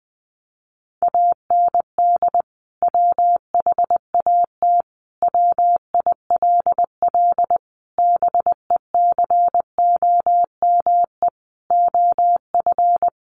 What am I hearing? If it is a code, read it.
Morse code, heard as a keyed tone, about 20 wpm, 700 Hz. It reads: AND WHAT WILL BECOME OF
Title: Sentences from Top 600 Words 20wpm